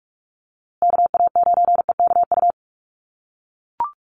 Morse code, heard as a keyed tone, about 44 wpm, 700 Hz. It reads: KA9EKU
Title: USA Call Signs Repeated 2x 45wpm